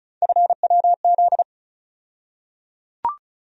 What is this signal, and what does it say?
Morse code, heard as a keyed tone, about 36 wpm, 700 Hz. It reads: FWZ
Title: USA Call Sign Postfixes - 3 Letters 35wpm